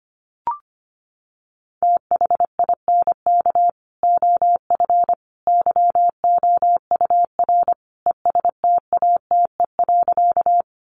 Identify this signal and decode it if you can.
Morse code, heard as a keyed tone, about 25 wpm, 700 Hz. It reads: THINK OF YOUR ESTATE.
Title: Sentences from Top 1200 Words 25wpm